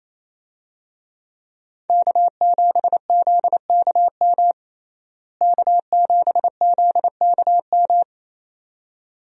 Morse code, heard as a keyed tone, about 28 wpm, 700 Hz. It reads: K7ZKM K7ZKM
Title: USA Call Signs - 5 Characters - Repeated 2x - 28wpm